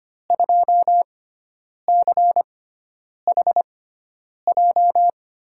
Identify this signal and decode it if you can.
Morse code, heard as a keyed tone, about 25 wpm, spaced as for 9 wpm, 700 Hz. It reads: 2CHJ
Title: USA Call Signs - 6 Characters - 6x Character Spacing - 25wpm